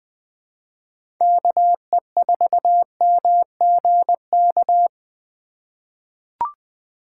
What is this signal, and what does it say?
Morse code, heard as a keyed tone, about 20 wpm, 700 Hz. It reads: KE4MGK
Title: USA Call Signs - 4x Character Spacing - 20wpm